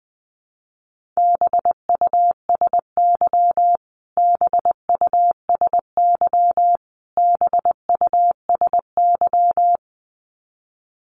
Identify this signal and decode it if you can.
Morse code, heard as a keyed tone, about 20 wpm, 700 Hz. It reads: BUSY BUSY BUSY
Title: Top 500 Words Repeated 20wpm